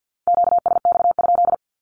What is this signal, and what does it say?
Morse code, heard as a keyed tone, about 51 wpm, 700 Hz. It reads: QSX?
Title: Q-Codes Repeated 3x 50wpm